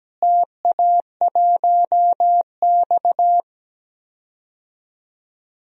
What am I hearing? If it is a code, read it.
Morse code, heard as a keyed tone, about 17 wpm, 700 Hz. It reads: TA1X
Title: DXCC-100 Call Signs - 4 Characters - Repeated 3x - 17wpm